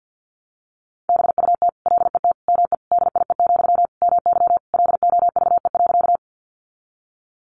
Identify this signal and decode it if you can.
Morse code, heard as a keyed tone, about 50 wpm, 700 Hz. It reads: BUT LET ME DIE, MY LOVE.
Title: Sentences from Top 1300 Words 50wpm